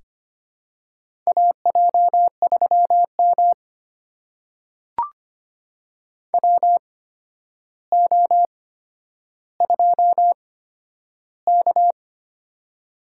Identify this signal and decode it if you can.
Morse code, heard as a keyed tone, about 25 wpm, spaced as for 11 wpm, 700 Hz. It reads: AJ3M WO2K
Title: USA Call Signs - 4 Characters - 8x Character Spacing - 25wpm